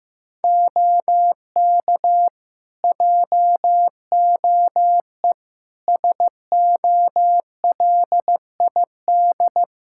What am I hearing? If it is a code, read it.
Morse code, heard as a keyed tone, about 15 wpm, 700 Hz. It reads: OK JOE SOLID
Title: K1USN SST QSO Practice 15wpm